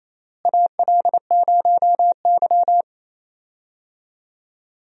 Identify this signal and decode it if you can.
Morse code, heard as a keyed tone, about 28 wpm, 700 Hz. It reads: AL0Y
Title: USA Call Signs - 4 Characters - Repeated 3x - 28wpm